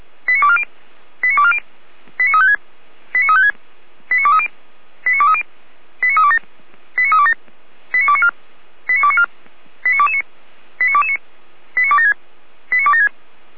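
Five-tone selective call (ZVEI), heard as quick runs of five tones, tones of about 70 ms. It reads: 89240 89240 89356 89356 89230 89230 89327 89327 89294 89294 89229 89229 89276 89276
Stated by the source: Five-tone ZVEI selective call received by SDR.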